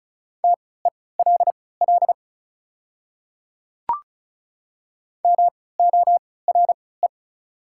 Morse code, heard as a keyed tone, about 36 wpm, spaced as for 20 wpm, 690 Hz. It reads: TELL MORE
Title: 3x Character Spacing - Top 300 Words - 35wpm